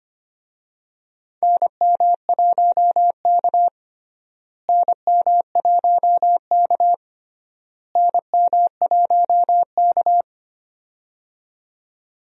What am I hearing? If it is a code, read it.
Morse code, heard as a keyed tone, about 25 wpm, 700 Hz. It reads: NM1K NM1K NM1K
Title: USA Call Signs - 4 Characters - Repeated 3x - 25wpm